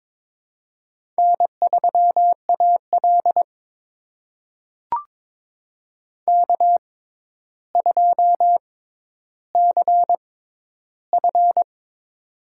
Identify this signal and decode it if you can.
Morse code, heard as a keyed tone, about 22 wpm, spaced as for 11 wpm, 700 Hz. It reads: N3AL K2CF
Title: USA Call Signs - 4 Characters - 6x Character Spacing - 22wpm